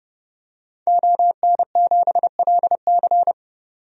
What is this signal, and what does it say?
Morse code, heard as a keyed tone, about 31 wpm, 700 Hz. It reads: ON7LC
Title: DXCC-100 Call Signs - 5 Characters - 30wpm